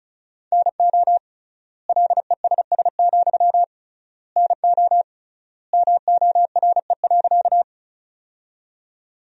Morse code, heard as a keyed tone, about 36 wpm, 700 Hz. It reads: NO LESS, NO MORE.
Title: 3x Word-Spacing - Sentences from Top 400 Words - 35wpm